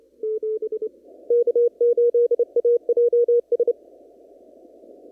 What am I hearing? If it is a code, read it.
Morse code, heard as a keyed tone, about 30 wpm, 1430 Hz. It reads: K8AJS
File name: CW / kt5j wpx cw-op-k5pi